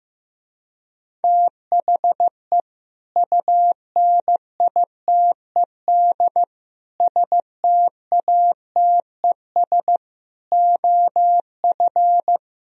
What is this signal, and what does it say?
Morse code, heard as a keyed tone, about 15 wpm, 700 Hz. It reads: THE UNITED STATES OF
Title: Speed-Racing - Sets of 5 Words 15wpm